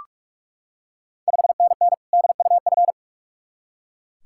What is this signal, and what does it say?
Morse code, heard as a keyed tone, about 44 wpm, 690 Hz. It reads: 5NN DUF
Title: Ontario QSO Party Exchange 45wpm